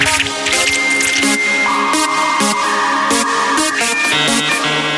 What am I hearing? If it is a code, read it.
Dial-up modem handshake.